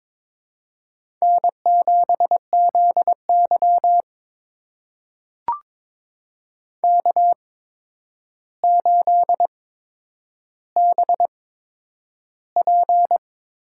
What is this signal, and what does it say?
Morse code, heard as a keyed tone, about 22 wpm, spaced as for 9 wpm, 700 Hz. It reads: N7ZY K8BP
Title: USA Call Signs - 8x Character Spacing - 22wpm